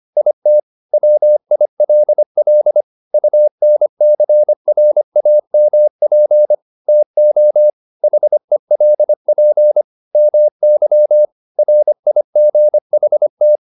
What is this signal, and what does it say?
Morse code, heard as a keyed tone, about 25 wpm, 600 Hz. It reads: IT WILL UNCRAMP TO HELP MY RIGHT
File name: the old man and the sea part 1 25wpm